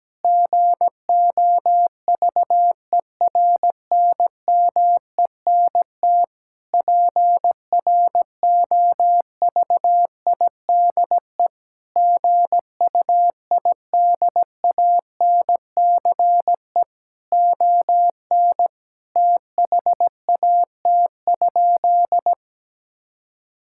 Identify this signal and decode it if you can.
Morse code, heard as a keyed tone, about 17 wpm, 700 Hz. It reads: GOVERNMENT PROVIDE GUIDANCE ON THAT?
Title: Sentences from Top 1600 Words 17wpm